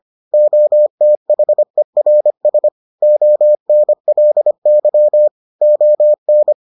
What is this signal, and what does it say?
Morse code, heard as a keyed tone, about 25 wpm, 600 Hz. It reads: OTHERS ONLY ON